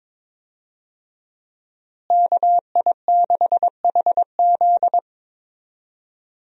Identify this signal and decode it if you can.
Morse code, heard as a keyed tone, about 22 wpm, 700 Hz. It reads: KI6HZ
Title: USA Call Signs - 5 Characters - Repeated 3x - 22wpm